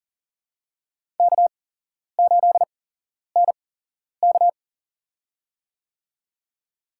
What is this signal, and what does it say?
Morse code, heard as a keyed tone, about 40 wpm, spaced as for 11 wpm, 700 Hz. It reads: K8NK
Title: USA Call Signs - 8x Character Spacing - 40wpm